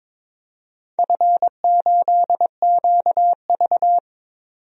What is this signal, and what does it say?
Morse code, heard as a keyed tone, about 22 wpm, 700 Hz. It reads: F8QV
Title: DXCC-100 Call Signs - 4 Characters - 22wpm